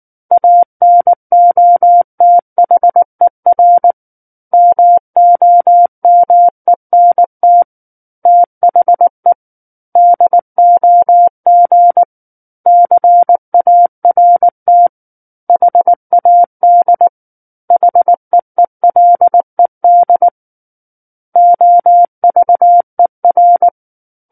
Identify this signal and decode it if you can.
Morse code, heard as a keyed tone, about 19 wpm, 700 Hz. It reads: ANOTHER MOMENT THE DOG CART HAD HEELED OVER
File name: Book One / 10-Chapter 10 - 19 WPM